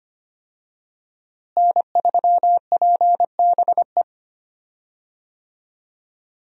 Morse code, heard as a keyed tone, about 25 wpm, 700 Hz. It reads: N3PBE
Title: USA Call Signs - 5 Characters - Repeated 3x - 25wpm